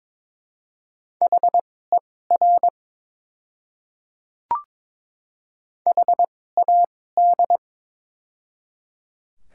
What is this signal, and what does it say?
Morse code, heard as a keyed tone, about 22 wpm, spaced as for 16 wpm, 700 Hz. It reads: HER HAD
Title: 2x Character Spacing - Top 100 Words - 22wpm